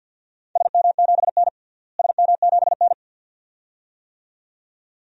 Morse code, heard as a keyed tone, about 47 wpm, 700 Hz. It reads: SM7N SM7N
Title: DXCC-100 Call Signs - 4 Characters - Repeated 2x - 50wpm